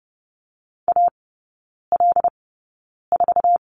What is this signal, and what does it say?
Morse code, heard as a keyed tone, about 30 wpm, spaced as for 9 wpm, 700 Hz. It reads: AL4
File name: course / 30:10wpm / 087 Lesson 15e - Number 5 - Three Characters 30wpm